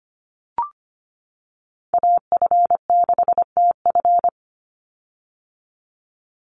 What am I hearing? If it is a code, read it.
Morse code, heard as a keyed tone, about 25 wpm, 700 Hz. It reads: AF6TF
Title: USA Call Signs 25wpm